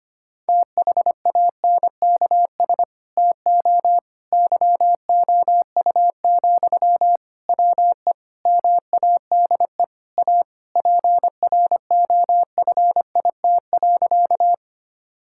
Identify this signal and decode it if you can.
Morse code, heard as a keyed tone, about 25 wpm, 700 Hz. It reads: THANKS TO YOU, WE MADE A PROFIT.
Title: Sentences from Top 1000 Words - Review Part 5 - 25wpm